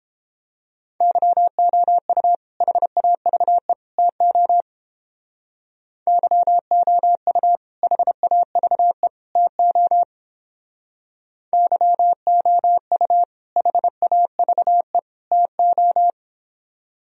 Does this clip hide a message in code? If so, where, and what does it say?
Yes: Morse code from 1.0 to 16.1 s, heard as a keyed tone, about 29 wpm, 700 Hz. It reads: YOU HAVE TO YOU HAVE TO YOU HAVE TO